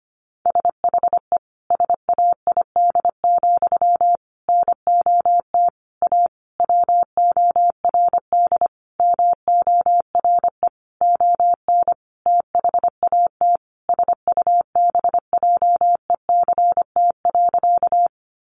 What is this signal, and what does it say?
Morse code, heard as a keyed tone, about 25 wpm, 700 Hz. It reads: SHE SAID, NOT A WORD MORE ON THAT SUBJECT.
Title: Sentences from Top 800 Words 25wpm